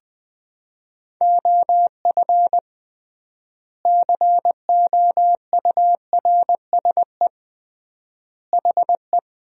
Morse code, heard as a keyed tone, about 21 wpm, 700 Hz. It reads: OF COURSE HE
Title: 3x Word-Spacing - Sentences from Top 500 Words - 20wpm